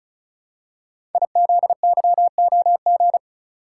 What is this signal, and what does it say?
Morse code, heard as a keyed tone, about 36 wpm, 700 Hz. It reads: IZYOG
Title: Code Groups - 5 Letters - 35wpm